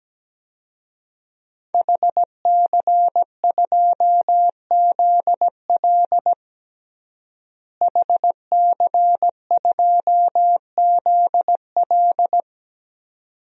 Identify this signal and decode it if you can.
Morse code, heard as a keyed tone, about 17 wpm, 700 Hz. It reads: HC2ZL HC2ZL
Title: DXCC-100 Call Signs - 5 Characters - Repeated 2x - 17wpm